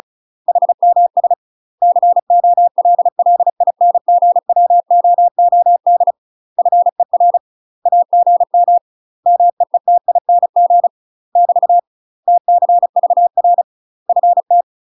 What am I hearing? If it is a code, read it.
Morse code, heard as a keyed tone, about 36 wpm, 700 Hz. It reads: HMS COLLINGWOOD FER AGM MEETING <BT> TCVR FT
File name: UK MASTER QSOs TRAINING / 35WPM / UK MASTER QSOs 0016